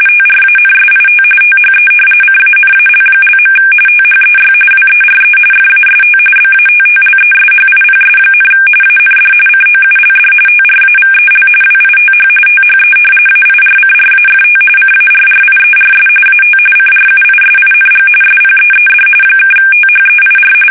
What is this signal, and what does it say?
RTTY, about 45.45 baud. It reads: TRANSIT THE CANAL). THEY WERE FAST BATTLESHIPS, AND COULD TRAVEL WITH THE AIRCRAFT CARRIERS AT CRUISING SPEED (THEIR
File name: fsk850 / battleships